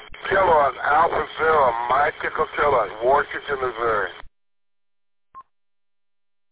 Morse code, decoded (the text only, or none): none